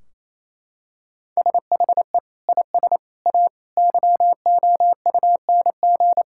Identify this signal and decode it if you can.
Morse code, heard as a keyed tone, about 28 wpm, 700 Hz. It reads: SHE IS A YOUNG
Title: Sentences from Top 1600 Words 28wpm